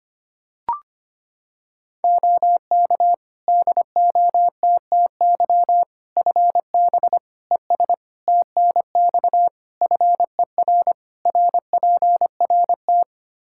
Morse code, heard as a keyed tone, about 25 wpm, 700 Hz. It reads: OK DOTTY FB ES TNX FER RPRT
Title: K1USN SST QSO Practice 25wpm